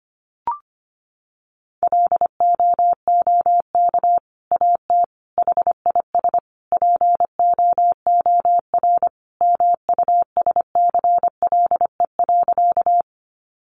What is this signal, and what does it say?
Morse code, heard as a keyed tone, about 25 wpm, 700 Hz. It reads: LOOK AT HIS POOR MUSCLE.
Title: Sentences from Top 1300 Words 25wpm